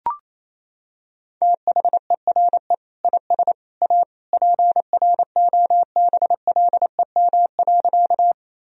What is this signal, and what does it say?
Morse code, heard as a keyed tone, about 28 wpm, 700 Hz. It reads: THERE IS A PROBLEM.
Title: Sentences from Top 1000 Words - Review Part 2 - 28wpm